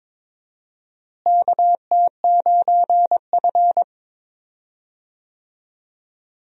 Morse code, heard as a keyed tone, about 22 wpm, 700 Hz. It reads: KT9F
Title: USA Call Signs - 4 Characters - Repeated 2x - 22wpm